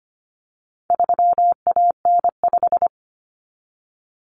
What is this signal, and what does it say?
Morse code, heard as a keyed tone, about 25 wpm, 700 Hz. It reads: 3AN5
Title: Common Cut Numbers - 4-Digit 25wpm